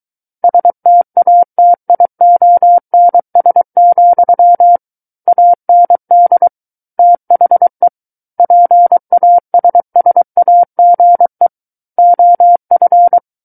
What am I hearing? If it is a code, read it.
Morse code, heard as a keyed tone, about 23 wpm, 700 Hz. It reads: STATIONS, AND THE PASSAGE OF
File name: Book One / 14-Chapter 14 - 23 WPM